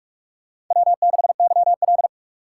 Morse code, heard as a keyed tone, about 47 wpm, 700 Hz. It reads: W6YL
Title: USA Call Signs - 4 Characters - 3x Character Spacing - 45wpm